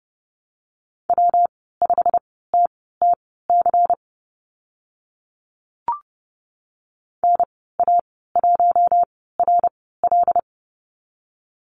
Morse code, heard as a keyed tone, about 31 wpm, spaced as for 17 wpm, 700 Hz. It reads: W5TTC NA1RL
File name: Lesson 19g - Letter w - Call Signs 30wpm